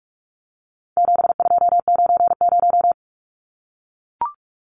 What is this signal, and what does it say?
Morse code, heard as a keyed tone, about 44 wpm, 700 Hz. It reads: 7290
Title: Numbers - 4-Digit 45wpm